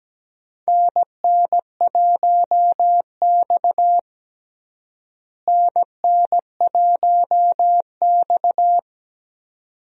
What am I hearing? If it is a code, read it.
Morse code, heard as a keyed tone, about 17 wpm, 700 Hz. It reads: NN1X NN1X
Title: USA Call Signs - 4 Characters - Repeated 2x - 17wpm